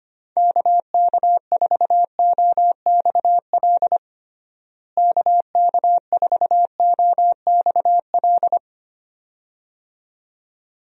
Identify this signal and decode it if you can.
Morse code, heard as a keyed tone, about 25 wpm, 700 Hz. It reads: KK4OXL KK4OXL
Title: USA Call Signs - 6 Characters - Repeated 2x - 25wpm